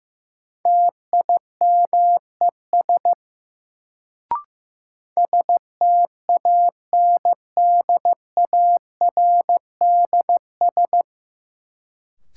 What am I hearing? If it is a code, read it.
Morse code, heard as a keyed tone, about 15 wpm, 700 Hz. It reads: TIMES STANDARDS
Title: Sentences from Top 1000 Words - Review Part 6 - New Word Drill - 15wpm